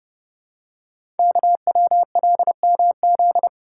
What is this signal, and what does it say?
Morse code, heard as a keyed tone, about 30 wpm, 700 Hz. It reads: KWLMZ
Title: Code Groups - 5 Letter:Numbers - 30wpm